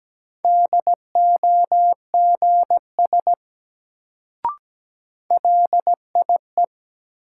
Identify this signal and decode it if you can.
Morse code, heard as a keyed tone, about 17 wpm, 700 Hz. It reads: DOGS LIE
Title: Sentences from Top 1600 Words - New Word Drill Alt - 17wpm